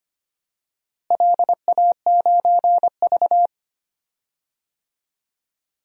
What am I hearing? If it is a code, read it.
Morse code, heard as a keyed tone, about 25 wpm, 700 Hz. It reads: LA9V
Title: DXCC-100 Call Signs - 4 Characters - Repeated 3x - 25wpm